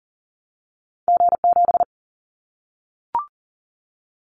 Morse code, heard as a keyed tone, about 40 wpm, 700 Hz. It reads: G7
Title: Callsign Prefixes for Top 25 DXCC - 40wpm